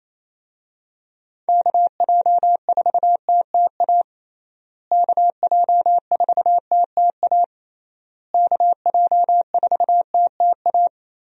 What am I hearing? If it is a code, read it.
Morse code, heard as a keyed tone, about 27 wpm, 700 Hz. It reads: KJ4TTA KJ4TTA KJ4TTA
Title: USA Call Signs - 6 Characters - Repeated 3x - 28wpm